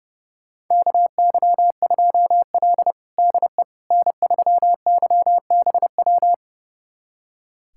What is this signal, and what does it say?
Morse code, heard as a keyed tone, about 32 wpm, 700 Hz. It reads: KY2L DE N3YBW
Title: USA Callsign DE Callsign 30wpm